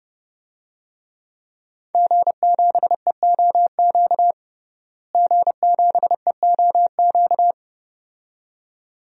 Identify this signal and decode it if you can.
Morse code, heard as a keyed tone, about 30 wpm, 700 Hz. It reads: G7EOQ G7EOQ
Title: DXCC-100 Call Signs - 5 Characters - Repeated 2x - 30wpm